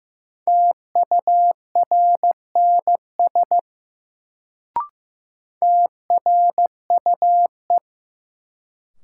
Morse code, heard as a keyed tone, about 15 wpm, 700 Hz. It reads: TURNS TRUE
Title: Sets of 5 Words Encore - New Word Drill - 15wpm